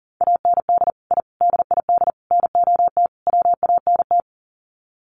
Morse code, heard as a keyed tone, about 40 wpm, 700 Hz. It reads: AND I DID NOT WANT
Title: Sets of 5 Words 40wpm